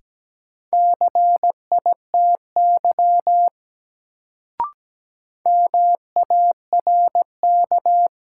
Morse code, heard as a keyed tone, about 17 wpm, 700 Hz. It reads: CITY MARK
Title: Sentences from Top 300 Words - New Word Drill - 17wpm